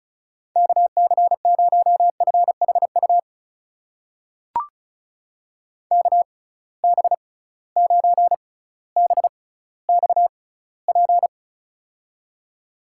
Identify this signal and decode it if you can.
Morse code, heard as a keyed tone, about 34 wpm, spaced as for 18 wpm, 700 Hz. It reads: KC0FHU KB9BXP
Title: USA Call Signs - 6 Characters - 6x Character Spacing - 35wpm